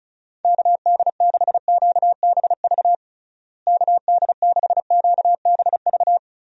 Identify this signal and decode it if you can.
Morse code, heard as a keyed tone, about 34 wpm, 700 Hz. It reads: KD6QBV KD6QBV
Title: USA Call Signs - 6 Characters - Repeated 2x - 35wpm